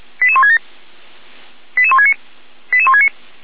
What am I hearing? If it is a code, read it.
Five-tone selective call (ZVEI), heard as quick runs of five tones, tones of about 70 ms. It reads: 88167 88169 88169
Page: /455/SDR/ZVEI/88163/